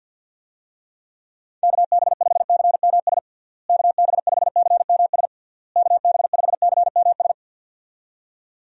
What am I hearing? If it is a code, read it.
Morse code, heard as a keyed tone, about 52 wpm, 700 Hz. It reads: KB5CMS KB5CMS KB5CMS
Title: USA Call Signs - 6 Characters - Repeated 3x - 50wpm